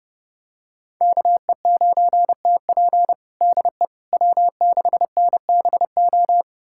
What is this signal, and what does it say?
Morse code, heard as a keyed tone, about 30 wpm, 700 Hz. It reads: KE9TP DE W6NBO
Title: USA Callsign DE Callsign - 5 Characters - 30wpm